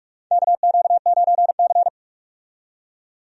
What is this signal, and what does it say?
Morse code, heard as a keyed tone, about 46 wpm, 700 Hz. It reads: KQ9C
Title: USA Call Signs - 4 Characters - 45wpm